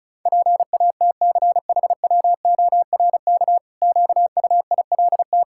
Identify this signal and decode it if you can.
Morse code, heard as a keyed tone, about 36 wpm, 700 Hz. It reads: PATCHWORK QUILT
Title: Pangram Challenge v2 - 35wpm